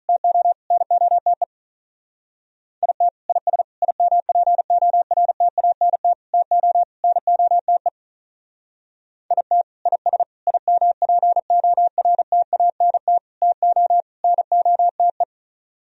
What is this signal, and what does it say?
Morse code, heard as a keyed tone, about 44 wpm, 700 Hz. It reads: TO NOTE IT IS IMPORTANT TO NOTE IT IS IMPORTANT TO NOTE